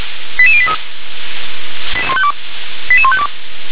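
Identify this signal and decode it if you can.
Five-tone selective call (ZVEI), heard as quick runs of five tones, tones of about 70 ms. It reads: 88083 88152
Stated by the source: ZVEI five-tone selective call received by SDR.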